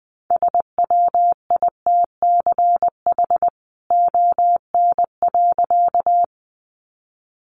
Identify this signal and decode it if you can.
Morse code, heard as a keyed tone, about 20 wpm, 700 Hz. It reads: SWITCH ON.
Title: A Sound Of Thunder 20wpm